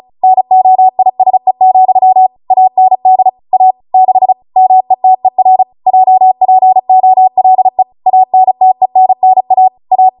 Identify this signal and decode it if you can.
Morse code, heard as a keyed tone, about 36 wpm, 750 Hz. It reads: NOISE, AND A 6 METER JPOLE ANTENNA A